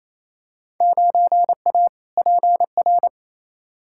Morse code, heard as a keyed tone, about 28 wpm, 700 Hz. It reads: 9A PR
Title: Field Day Exchanges Encore - 28wpm